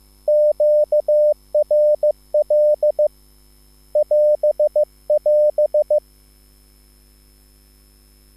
Morse code, heard as a keyed tone, about 15 wpm, 600 Hz. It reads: QRL <AS><AS>
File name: QSL / CW lesson